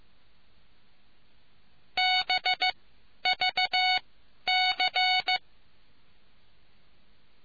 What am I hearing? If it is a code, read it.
Morse code, heard as a keyed tone, about 15 wpm, spaced as for 10 wpm, 740 Hz. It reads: BVC